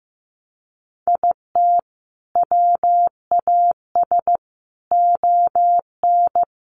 Morse code, heard as a keyed tone, about 15 wpm, 700 Hz. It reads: IT WAS ON
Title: Sentences from Top 800 Words 15wpm